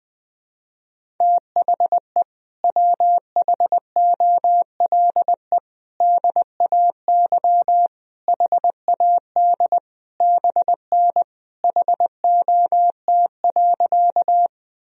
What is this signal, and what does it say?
Morse code, heard as a keyed tone, about 20 wpm, 700 Hz. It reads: THE WHOLE DAY HAD BN HOT.
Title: Sentences from Top 500 Words and Abbreviations 20wpm